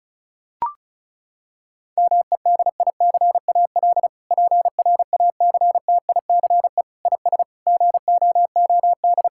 Morse code, heard as a keyed tone, about 34 wpm, 700 Hz. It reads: MEDICAL PRACTICE IS GOOD
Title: Sentences from Top 1000 Words - Review Part 5 - 35wpm